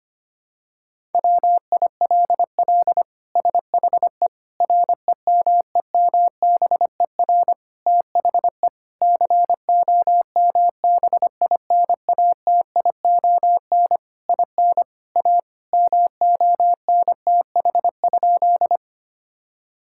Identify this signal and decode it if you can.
Morse code, heard as a keyed tone, about 25 wpm, 700 Hz. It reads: WILL SHE REMEMBER THE COMBINATION IN A MONTH?
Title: Sentences from Top 1000 Words - Review Part 6 - 25wpm